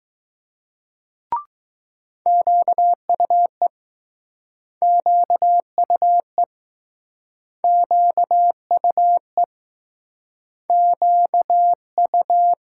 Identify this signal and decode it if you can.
Morse code, heard as a keyed tone, about 20 wpm, 700 Hz. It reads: QUE QUE QUE QU